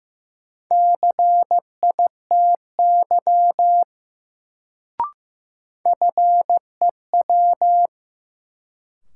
Morse code, heard as a keyed tone, about 15 wpm, 700 Hz. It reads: CITY FEW
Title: Sentences from Top 300 Words - New Word Drill - 15wpm